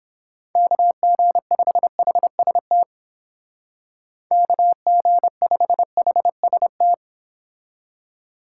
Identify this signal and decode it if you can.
Morse code, heard as a keyed tone, about 29 wpm, 700 Hz. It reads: KG5HST KG5HST